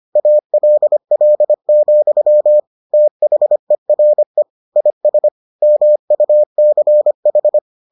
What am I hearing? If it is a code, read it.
Morse code, heard as a keyed tone, about 25 wpm, 600 Hz. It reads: ALL, THERE IS MUCH